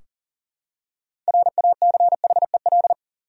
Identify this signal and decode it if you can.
Morse code, heard as a keyed tone, about 38 wpm, 700 Hz. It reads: RACHEL
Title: USA Common Names - 40wpm